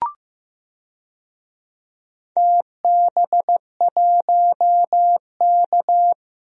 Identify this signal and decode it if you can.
Morse code, heard as a keyed tone, about 16 wpm, 700 Hz. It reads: TB1K
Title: DXCC-100 Call Signs - 4 Characters - Repeated 2x - 15wpm